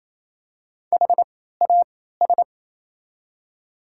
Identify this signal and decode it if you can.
Morse code, heard as a keyed tone, about 28 wpm, spaced as for 16 wpm, 700 Hz. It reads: HAS
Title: 3x Character Spacing - Top 500 Words - 28wpm